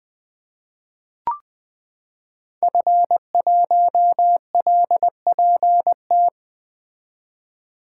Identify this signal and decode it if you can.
Morse code, heard as a keyed tone, about 20 wpm, 700 Hz. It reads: F1LPT
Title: 2023 POTA Activator Call Signs - 20wpm